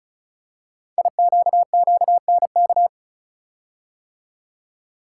Morse code, heard as a keyed tone, about 33 wpm, 690 Hz. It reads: IQQNK
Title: Code Groups - 5 Letters - 35wpm